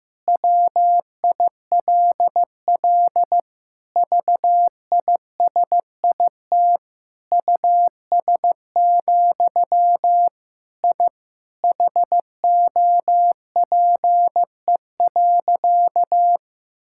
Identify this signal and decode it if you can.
Morse code, heard as a keyed tone, about 15 wpm, 700 Hz. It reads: WILL VISIT US, I HOPE.
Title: Sentences from Top 1000 Words - Review Part 6 - 15wpm